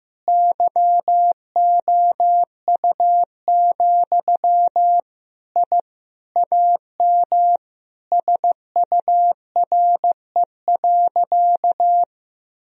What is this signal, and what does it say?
Morse code, heard as a keyed tone, about 15 wpm, 700 Hz. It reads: YOU, I AM SURE.
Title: Sentences from Top 1000 Words - Review Part 3 - 15wpm